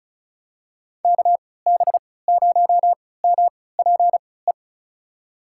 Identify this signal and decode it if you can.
Morse code, heard as a keyed tone, about 36 wpm, spaced as for 20 wpm, 700 Hz. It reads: KB0MPE
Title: USA Call Signs - 6 Characters - 3x Character Spacing - 35wpm